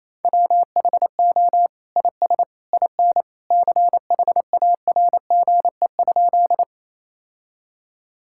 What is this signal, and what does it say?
Morse code, heard as a keyed tone, about 28 wpm, 700 Hz. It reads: WHO IS IN CHARGE?
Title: Sentences from Top 1000 Words - Review Part 6 - 28wpm